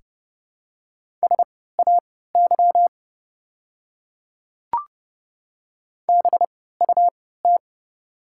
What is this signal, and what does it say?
Morse code, heard as a keyed tone, about 30 wpm, spaced as for 17 wpm, 690 Hz. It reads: SAY BUT
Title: 3x Character Spacing - Top 200 Words - 30wpm